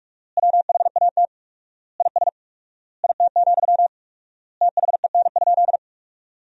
Morse code, heard as a keyed tone, about 45 wpm, 700 Hz. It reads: WHAT IS IT, THEN?
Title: 4x Word-Spacing - Sentences from Top 200 Words - 45wpm